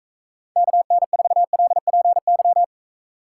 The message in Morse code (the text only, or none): KN4LPY